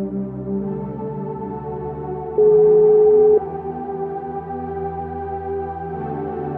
Ringback tone (Europe, 425 Hz).